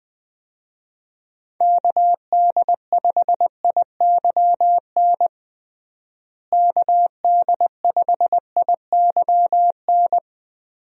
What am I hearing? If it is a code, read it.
Morse code, heard as a keyed tone, about 20 wpm, 700 Hz. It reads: KD5IYN KD5IYN
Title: USA Call Signs - 6 Characters - Repeated 3x - 20wpm